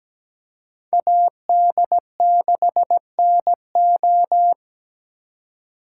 Morse code, heard as a keyed tone, about 21 wpm, 700 Hz. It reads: AD6NO
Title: Speed Racing - USA 2x2 Call Signs 17wpm